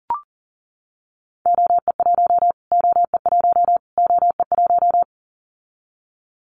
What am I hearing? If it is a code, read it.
Morse code, heard as a keyed tone, about 40 wpm, 700 Hz. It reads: OE1 OE1 OE1
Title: Callsign Prefixes for Top 25 DXCC - Repeated 3x - 40wpm